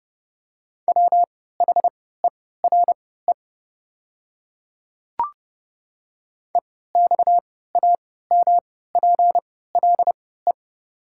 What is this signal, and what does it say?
Morse code, heard as a keyed tone, about 29 wpm, spaced as for 17 wpm, 690 Hz. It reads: WHERE EXAMPLE
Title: 3x Character Spacing - Top 300 Words - 30wpm